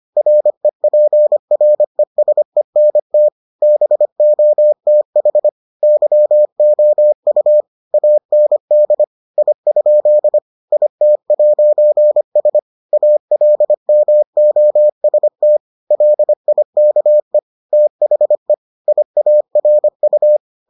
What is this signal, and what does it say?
Morse code, heard as a keyed tone, about 25 wpm, 600 Hz. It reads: REPRESENT BOTH YOU AND I? IT'S ALMOST LIKE THE IARU